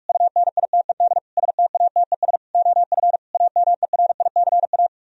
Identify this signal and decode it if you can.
Morse code, heard as a keyed tone, about 44 wpm, 700 Hz. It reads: UNITED STATES OF AMERICA